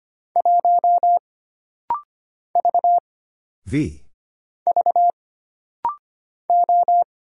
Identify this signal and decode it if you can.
Morse code, heard as a keyed tone, about 25 wpm, 700 Hz. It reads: 1 V V O
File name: Single Character (Letter:Number:Punctuation:Prosign) - Rapid-Fire - 25wpm